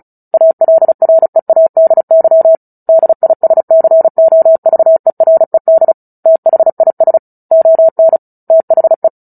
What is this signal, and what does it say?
Morse code, heard as a keyed tone, about 34 wpm, 650 Hz. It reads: ALREADY DISCOVERED THIS ON THE
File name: Book Two / 09-Chapter 9 - 35 WPM